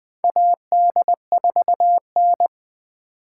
Morse code, heard as a keyed tone, about 20 wpm, 700 Hz. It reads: AD4N
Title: USA Call Signs - 4 Characters - 6x Character Spacing - 20wpm